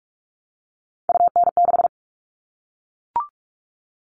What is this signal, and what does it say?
Morse code, heard as a keyed tone, about 44 wpm, 700 Hz. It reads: UN6
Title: All Cut Numbers - 3-Digit 45wpm